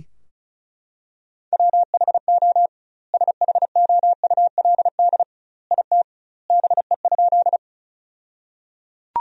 Morse code, heard as a keyed tone, about 36 wpm, 700 Hz. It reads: WHO SHOULD IT BE?
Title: 2x Word-Spacing - Sentences from Top 300 Words - 35wpm